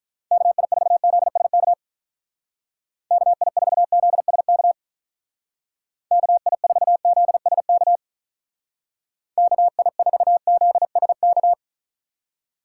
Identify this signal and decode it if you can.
Morse code, heard as a keyed tone, about 44 wpm, 700 Hz. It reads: KI4ZSK KI4ZSK KI4ZSK KI4ZSK